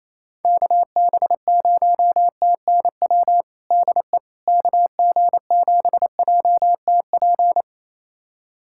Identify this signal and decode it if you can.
Morse code, heard as a keyed tone, about 28 wpm, 700 Hz. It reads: KB0TNW DE KG7JTP
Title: USA Callsign DE Callsign - 6 Characters - 28wpm